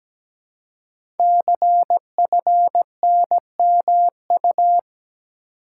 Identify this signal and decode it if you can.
Morse code, heard as a keyed tone, about 17 wpm, 700 Hz. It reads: CFNMU
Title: Code Groups - 5 Letters - 17wpm